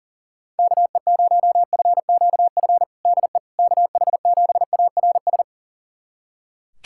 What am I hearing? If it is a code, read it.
Morse code, heard as a keyed tone, about 37 wpm, 700 Hz. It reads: KE0FQF DE KH7ARS
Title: USA Callsign DE Callsign - 6 Characters - 40wpm